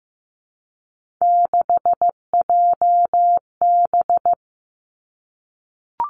Morse code, heard as a keyed tone, about 15 wpm, 700 Hz. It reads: 6JB
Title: Three Challenging Letter-Number Combinations 15wpm